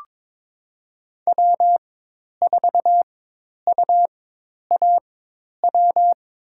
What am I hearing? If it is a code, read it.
Morse code, heard as a keyed tone, about 22 wpm, spaced as for 10 wpm, 700 Hz. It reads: W4UAW
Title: USA Call Signs - 4x Character Spacing - 22wpm